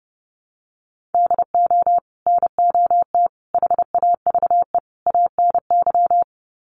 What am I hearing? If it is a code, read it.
Morse code, heard as a keyed tone, about 30 wpm, 700 Hz. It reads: DO NOT HAVE ANY
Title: Sets of 4 Words 30wpm